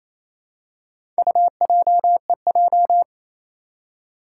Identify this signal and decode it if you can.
Morse code, heard as a keyed tone, about 28 wpm, 700 Hz. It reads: UJEJ
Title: Four Challenging Letter-Number Combinations 28wpm